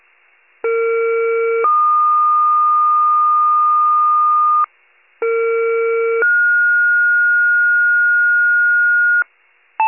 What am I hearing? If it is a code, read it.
Two-tone page, heard as a one-second tone, then a three-second tone, about 460 then 1220 Hz, about 460 then 1430 Hz, with noise. It is heard from 0.6 to 9.2 s.
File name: pager / 94 RESCUE OFC / wcalert